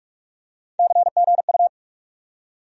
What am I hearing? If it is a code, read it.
Morse code, heard as a keyed tone, about 46 wpm, 700 Hz. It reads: CGU
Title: USA Call Sign Postfixes - 3 Letters 45wpm